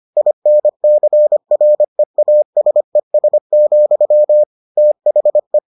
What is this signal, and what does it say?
Morse code, heard as a keyed tone, about 25 wpm, 600 Hz. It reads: INCREASES, THE